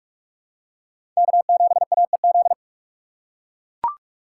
Morse code, heard as a keyed tone, about 40 wpm, 700 Hz. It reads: K7AEZ